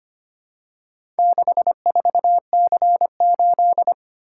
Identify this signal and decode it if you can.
Morse code, heard as a keyed tone, about 25 wpm, 700 Hz. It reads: 64C8
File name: practice / 25wpm / Four Letter-Number Combinations - 25wpm